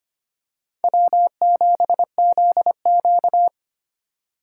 Morse code, heard as a keyed tone, about 25 wpm, 700 Hz. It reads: W7ZQ
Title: USA Call Signs - 4x Character Spacing - 25wpm